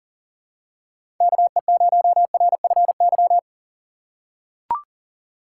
Morse code, heard as a keyed tone, about 40 wpm, 700 Hz. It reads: KE0RFY
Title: USA Call Signs - 6 Characters - 4x Character Spacing - 40wpm